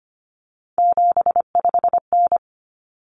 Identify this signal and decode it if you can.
Morse code, heard as a keyed tone, about 25 wpm, 700 Hz. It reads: 75N
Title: Common Cut Numbers - 3-Digit 25wpm